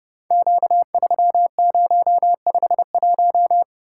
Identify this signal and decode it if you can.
Morse code, heard as a keyed tone, about 30 wpm, 700 Hz. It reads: Q3051
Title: Code Groups - 5 Letter:Numbers - 30wpm